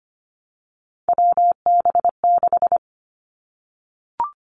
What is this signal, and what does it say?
Morse code, heard as a keyed tone, about 25 wpm, 700 Hz. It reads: WB6
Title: Three Challenging Letter-Number Combinations 25wpm